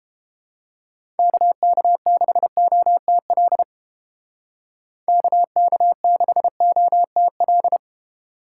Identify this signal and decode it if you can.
Morse code, heard as a keyed tone, about 29 wpm, 700 Hz. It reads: KK6OTL KK6OTL